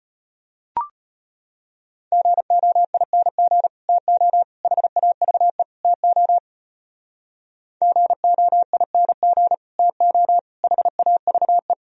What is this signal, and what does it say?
Morse code, heard as a keyed tone, about 30 wpm, 700 Hz. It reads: GOING TO HAVE TO GOING TO HAVE
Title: Speed-Racing - Sets of 4 Words 25wpm